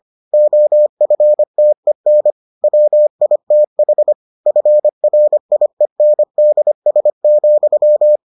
Morse code, heard as a keyed tone, about 25 wpm, 600 Hz. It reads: OFTEN WITH FRIENDS,